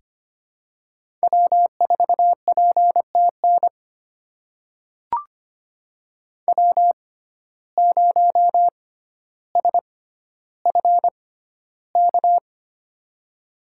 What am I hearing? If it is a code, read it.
Morse code, heard as a keyed tone, about 25 wpm, spaced as for 13 wpm, 700 Hz. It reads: W4PTN W0SFK
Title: USA Call Signs - 5 Characters - 6x Character Spacing - 25wpm